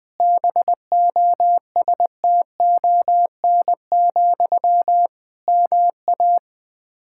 Morse code, heard as a keyed tone, about 20 wpm, 700 Hz. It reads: BOSTON, MA
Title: K1USN SST QSO Practice 20wpm